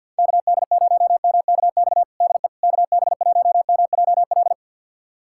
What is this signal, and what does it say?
Morse code, heard as a keyed tone, about 47 wpm, 700 Hz. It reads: KD0MKX DE KB1MJL
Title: USA Callsign DE Callsign - 6 Characters - 50wpm